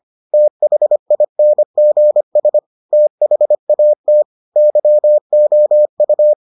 Morse code, heard as a keyed tone, about 25 wpm, 600 Hz. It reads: THINGS THAT YOU